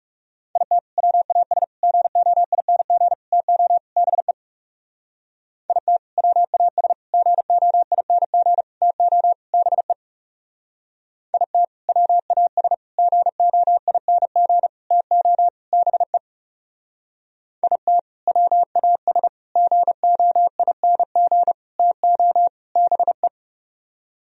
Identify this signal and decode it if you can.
Morse code, heard as a keyed tone, about 37 wpm, 700 Hz. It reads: IT WAS GOING TO BE IT WAS GOING TO BE IT WAS GOING TO BE IT WAS GOING TO BE
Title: Speed-Racing - Sets of 5 Words 30wpm